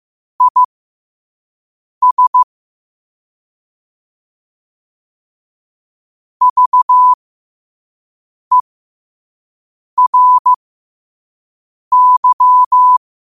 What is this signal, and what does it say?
Morse code, heard as a keyed tone, about 15 wpm, spaced as for 5 wpm, 1000 Hz. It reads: IS VERY
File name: qotdfiles / 5wpm